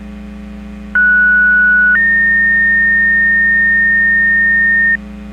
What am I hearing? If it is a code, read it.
Two-tone page, heard as a one-second tone, then a three-second tone, about 1470 then 1850 Hz, with noise. It is heard from 0.9 to 5.0 s.